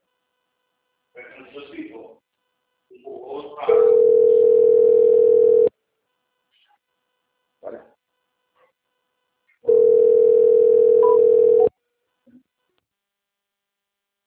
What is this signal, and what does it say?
Ringback tone (North America), 440 + 480 Hz, with speech.
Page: Atibaia - SP 1